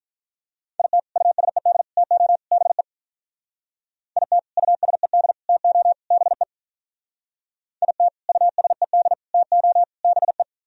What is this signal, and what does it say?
Morse code, heard as a keyed tone, about 44 wpm, 700 Hz. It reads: IT USED TO BE IT USED TO BE IT USED TO BE